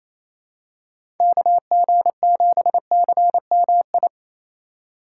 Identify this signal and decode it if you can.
Morse code, heard as a keyed tone, about 28 wpm, 700 Hz. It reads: KG7CMI
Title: USA Call Signs - 6 Characters - 4x Character Spacing - 28wpm